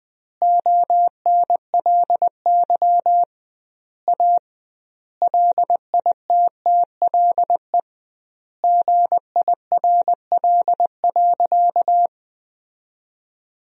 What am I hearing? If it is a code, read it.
Morse code, heard as a keyed tone, about 20 wpm, 700 Hz. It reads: ONLY A LITTLE GIRL.
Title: 2x Word-Spacing - Sentences from Top 400 Words - 20wpm